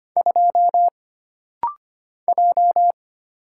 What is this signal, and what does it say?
Morse code, heard as a keyed tone, about 25 wpm, 700 Hz. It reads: 2 J
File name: Single Character (Letter:Number:Punctuation:Prosign) - Rapid-Fire - 25wpm